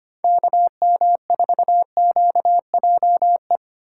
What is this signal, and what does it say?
Morse code, heard as a keyed tone, about 25 wpm, 700 Hz. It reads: KM4QJE
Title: USA Call Signs - 6 Characters - 4x Character Spacing - 25wpm